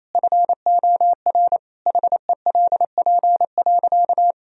Morse code, heard as a keyed tone, about 27 wpm, 700 Hz. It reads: FOR HELP.
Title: Sentences from Top 1000 Words - Review Part 4 - 28wpm